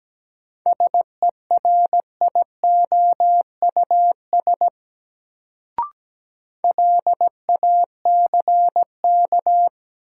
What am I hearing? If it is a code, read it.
Morse code, heard as a keyed tone, about 17 wpm, 700 Hz. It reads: SERIOUS LACK
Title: Sentences from Top 1100 Words - New Word Drill - 17wpm